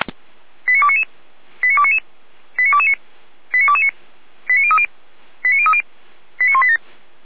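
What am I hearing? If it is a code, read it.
Five-tone selective call (ZVEI), heard as quick runs of five tones, tones of about 70 ms. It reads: 89299 89299 89229 89229 89030 89030 89187
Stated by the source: ZVEI five-tone call received by SDR.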